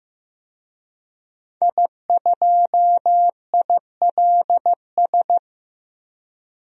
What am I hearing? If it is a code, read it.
Morse code, heard as a keyed tone, about 15 wpm, 700 Hz. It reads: I2ILS
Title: DXCC-100 Call Signs - 5 Characters - 15wpm